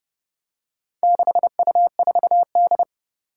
Morse code, heard as a keyed tone, about 32 wpm, 700 Hz. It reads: 6U4D
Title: Four Challenging Letter-Number Combinations 30wpm